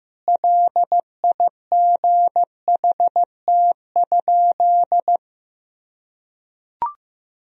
Morse code, heard as a keyed tone, about 16 wpm, 700 Hz. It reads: LIGHT?
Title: Sentences from Top 1000 Words - Review Part 1 - 15wpm